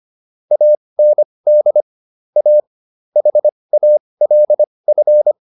Morse code, heard as a keyed tone, about 25 wpm, spaced as for 20 wpm, 600 Hz. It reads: AND A HALF